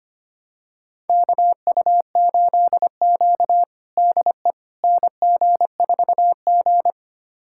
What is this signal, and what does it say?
Morse code, heard as a keyed tone, about 25 wpm, 700 Hz. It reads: KU8Q DE NG4G
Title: USA Callsign DE Callsign - 4 Characters - 25wpm